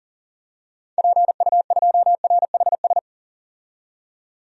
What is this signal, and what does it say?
Morse code, heard as a keyed tone, about 40 wpm, 700 Hz. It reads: PU2RHS
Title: DXCC-100 Call Signs - 6 Characters - 40wpm